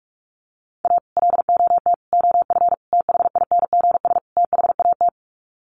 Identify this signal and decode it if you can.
Morse code, heard as a keyed tone, about 44 wpm, 700 Hz. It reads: A LOT OF THINGS THAT
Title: Sets of 5 Words 45wpm